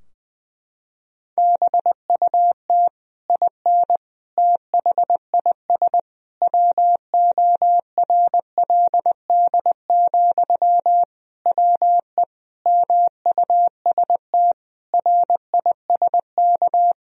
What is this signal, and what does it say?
Morse code, heard as a keyed tone, about 20 wpm, 700 Hz. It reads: BUT IN THIS WORLD, WE MUST RISK
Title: Sentences from Top 1000 Words - Review Part 4 - 20wpm